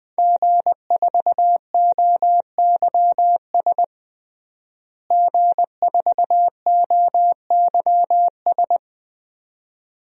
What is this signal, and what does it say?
Morse code, heard as a keyed tone, about 20 wpm, 700 Hz. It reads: G4OYS G4OYS
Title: DXCC-100 Call Signs - 5 Characters - Repeated 2x - 20wpm